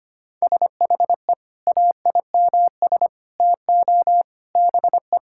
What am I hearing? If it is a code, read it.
Morse code, heard as a keyed tone, about 25 wpm, 700 Hz. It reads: SHE AIMS TO BE
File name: practice / 25wpm / Sentences with 9-Word Phrases - 25wpm